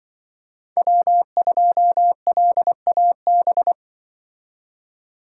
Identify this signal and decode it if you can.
Morse code, heard as a keyed tone, about 23 wpm, 700 Hz. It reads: W2LAB
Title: Speed Racing - USA 1x3 Call Signs 20wpm